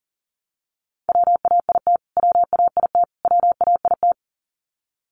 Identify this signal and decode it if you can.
Morse code, heard as a keyed tone, about 40 wpm, 700 Hz. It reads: WAIT WAIT WAIT
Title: Top 500 Words Repeated 40wpm